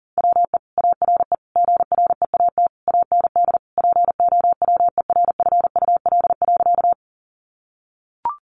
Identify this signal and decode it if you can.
Morse code, heard as a keyed tone, about 40 wpm, 700 Hz. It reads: WE ARE GREAT AND POWERFUL.
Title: Sentences from Top 1300 Words 40wpm